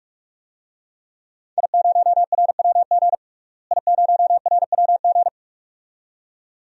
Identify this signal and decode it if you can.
Morse code, heard as a keyed tone, about 45 wpm, 700 Hz. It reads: I0RWG I0RWG
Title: DXCC-100 Call Signs - 5 Characters - Repeated 2x - 45wpm